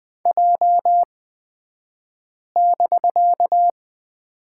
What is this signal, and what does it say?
Morse code, heard as a keyed tone, about 20 wpm, spaced as for 5 wpm, 700 Hz. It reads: J<BK>
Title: 249 Lesson 40d - BK - Two Characters 20wpm